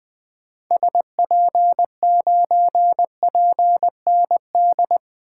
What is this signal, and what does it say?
Morse code, heard as a keyed tone, about 20 wpm, 700 Hz. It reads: SP9PND
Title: 2023 SOTA Activator Call Signs - 20wpm